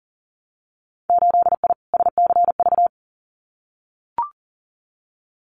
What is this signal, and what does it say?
Morse code, heard as a keyed tone, about 38 wpm, 700 Hz. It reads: 8I SCV
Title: Winter Field Day Exchanges Encore 40wpm